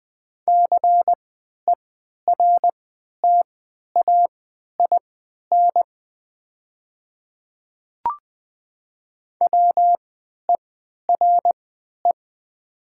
Morse code, heard as a keyed tone, about 20 wpm, spaced as for 11 wpm, 700 Hz. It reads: CERTAIN WERE
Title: 3x Character Spacing - Top 500 Words - 20wpm